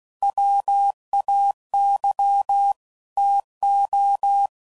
Morse code, heard as a keyed tone, about 16 wpm, 800 Hz. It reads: WAY TO